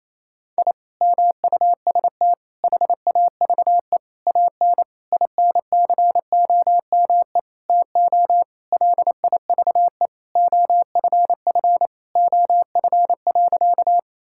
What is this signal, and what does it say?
Morse code, heard as a keyed tone, about 28 wpm, 700 Hz. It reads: I MUST HAVE AN INCOME TO LIVE OFF OF.
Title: Sentences from Top 1000 Words - Review Part 5 - 28wpm